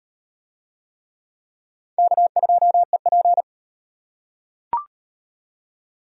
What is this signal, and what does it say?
Morse code, heard as a keyed tone, about 34 wpm, 700 Hz. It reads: K2EP